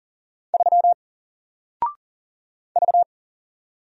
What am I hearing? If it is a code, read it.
Morse code, heard as a keyed tone, about 40 wpm, 700 Hz. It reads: 3 V
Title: Single Letter-Number 40wpm